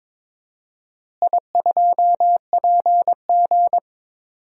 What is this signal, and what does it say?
Morse code, heard as a keyed tone, about 22 wpm, 700 Hz. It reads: I2PG
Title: DXCC-100 Call Signs - 4 Characters - Repeated 3x - 22wpm